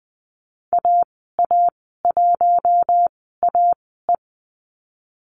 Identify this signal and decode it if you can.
Morse code, heard as a keyed tone, about 20 wpm, spaced as for 15 wpm, 700 Hz. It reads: AA1AE
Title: 062 Lesson 11g - Letter h - Call Signs 20wpm